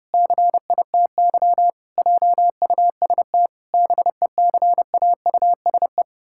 Morse code, heard as a keyed tone, about 30 wpm, 700 Hz. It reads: CITY JUST BECAUSE
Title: Pangram Challenge v2 - 30wpm